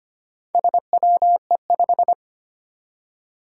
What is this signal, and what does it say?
Morse code, heard as a keyed tone, about 25 wpm, 700 Hz. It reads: SWE5
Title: Four Challenging Letter-Number Combinations 25wpm